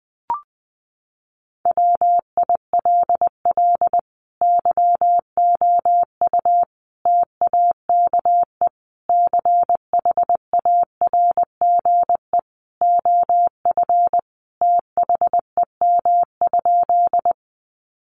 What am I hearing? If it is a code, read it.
Morse code, heard as a keyed tone, about 20 wpm, 700 Hz. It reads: WILL YOU TAKE CHARGE OF THEM?
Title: Sentences from Top 900 Words 20wpm